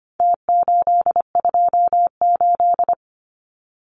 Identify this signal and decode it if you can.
Morse code, heard as a keyed tone, about 25 wpm, 700 Hz. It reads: T828
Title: Numbers - Contest Style - 4-Digit 25wpm